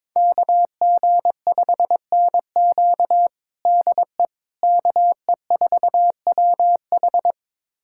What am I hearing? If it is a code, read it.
Morse code, heard as a keyed tone, about 22 wpm, 700 Hz. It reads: KG5NQ DE KE4WH
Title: USA Callsign DE Callsign 22wpm